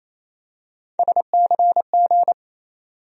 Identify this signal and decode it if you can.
Morse code, heard as a keyed tone, about 28 wpm, 700 Hz. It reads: SCG
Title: USA Call Sign Postfixes - 3 Letters 28wpm